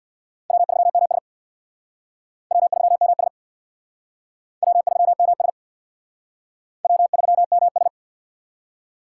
Morse code, heard as a keyed tone, about 60 wpm, 700 Hz. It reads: W3MS W3MS W3MS W3MS
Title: Speed Racing - USA 1x2 Call Signs 50wpm